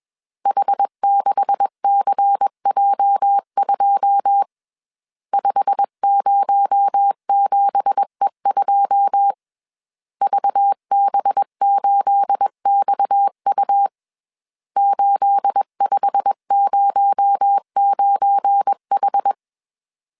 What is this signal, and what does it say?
Morse code, heard as a keyed tone, about 21 wpm, 800 Hz. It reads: H6CJ2 507E2 VB8XU 8509H
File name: CW / DET / 1 9 1500C S C 20wpm 2310